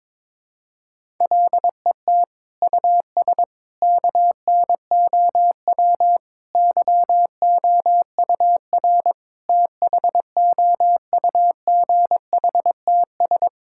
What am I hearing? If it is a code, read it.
Morse code, heard as a keyed tone, about 22 wpm, 700 Hz. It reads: LET US KNOW YOUR THOUGHTS
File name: Sets of 9 Words - 22wpm